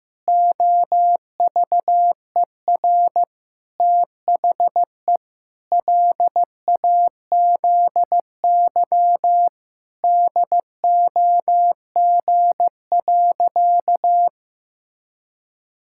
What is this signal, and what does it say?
Morse code, heard as a keyed tone, about 16 wpm, 700 Hz. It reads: OVER THE LAZY DOG.
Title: Pangram Challenge v2 - 15wpm